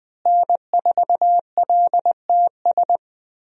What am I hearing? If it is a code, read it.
Morse code, heard as a keyed tone, about 21 wpm, 700 Hz. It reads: N4LTS
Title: USA Call Signs - 5 Characters - Repeated 2x - 20wpm